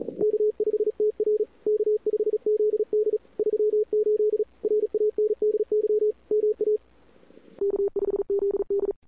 Morse code, heard as a keyed tone, about 38 wpm, 410 Hz. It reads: K5TR K5ZD 38 RANDY MA K5ZD
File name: CWSprint / Feb2007 / 2007-feb-cw-sprint-k5tr-op-n5rz -00:00